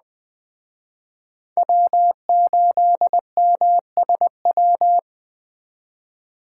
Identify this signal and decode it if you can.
Morse code, heard as a keyed tone, about 20 wpm, 700 Hz. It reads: W8MSW
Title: Speed Racing - USA Call Signs 17wpm